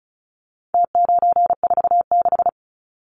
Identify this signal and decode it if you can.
Morse code, heard as a keyed tone, about 34 wpm, 700 Hz. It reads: T946
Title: Numbers - Contest Style - 4-Digit 35wpm